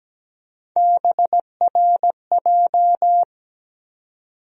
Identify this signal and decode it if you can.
Morse code, heard as a keyed tone, about 17 wpm, 700 Hz. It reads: BRJ
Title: USA Call Sign Postfixes - 3 Letters 17wpm